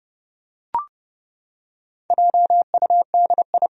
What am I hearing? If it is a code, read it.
Morse code, heard as a keyed tone, about 30 wpm, 700 Hz. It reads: JUDI
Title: USA Common Names - 30wpm